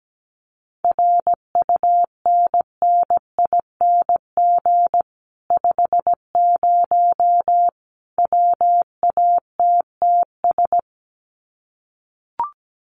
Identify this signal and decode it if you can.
Morse code, heard as a keyed tone, about 17 wpm, 700 Hz. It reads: RUNNING 50 WATTS
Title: Common QSO Phrases 17wpm